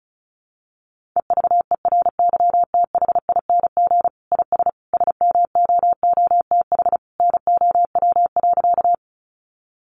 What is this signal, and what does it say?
Morse code, heard as a keyed tone, about 34 wpm, 700 Hz. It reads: EVERYTHING IS SMOOTH NOW.
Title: Sentences from Top 1400 Words 35wpm